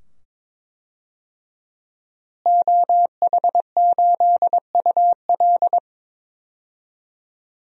Morse code, heard as a keyed tone, about 22 wpm, 700 Hz. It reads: OH8UL
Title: DXCC-100 Call Signs - 5 Characters - Repeated 2x - 22wpm